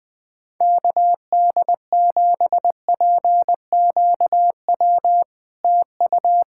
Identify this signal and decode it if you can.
Morse code, heard as a keyed tone, about 20 wpm, 700 Hz. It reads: KD7PQW TU
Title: SOTA QSO Practice 20wpm